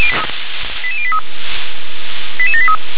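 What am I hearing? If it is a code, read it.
Five-tone selective call (ZVEI), heard as quick runs of five tones, tones of about 70 ms. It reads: 88083 88783
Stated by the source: Decoded by SDR as ZVEI five-tone selective call.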